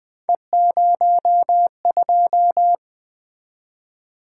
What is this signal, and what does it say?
Morse code, heard as a keyed tone, about 20 wpm, 700 Hz. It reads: E02
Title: Three Letter-Number Combinations - 20wpm